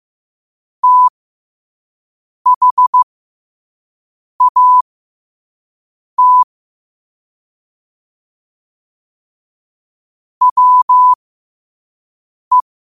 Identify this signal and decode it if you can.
Morse code, heard as a keyed tone, about 15 wpm, spaced as for 5 wpm, 1000 Hz. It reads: THAT WE
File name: qotdfiles / 5wpm